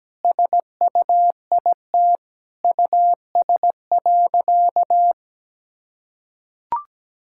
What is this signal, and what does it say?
Morse code, heard as a keyed tone, about 17 wpm, 700 Hz. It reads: SUIT US.
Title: Sentences from Top 1600 Words 17wpm